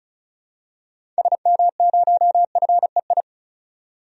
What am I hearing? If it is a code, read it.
Morse code, heard as a keyed tone, about 34 wpm, 700 Hz. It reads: SM0FEI
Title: DXCC-100 Call Signs - 6 Characters - 35wpm